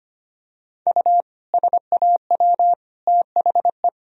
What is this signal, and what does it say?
Morse code, heard as a keyed tone, about 25 wpm, 700 Hz. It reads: U SAW THE
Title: Sentences from Top 400 Words and Abbreviations 25wpm